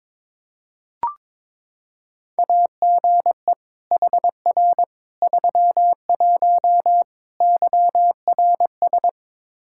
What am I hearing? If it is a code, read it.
Morse code, heard as a keyed tone, about 22 wpm, 700 Hz. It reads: AGE HR 31 YRS
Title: Practice QSO - Exchange 3 - More Info - 22wpm